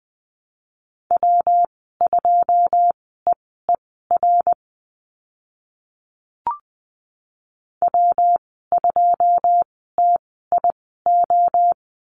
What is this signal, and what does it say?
Morse code, heard as a keyed tone, about 20 wpm, spaced as for 14 wpm, 700 Hz. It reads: W2EER W2TIO
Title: 122 Lesson 20g - Number 3 - Call Signs 20wpm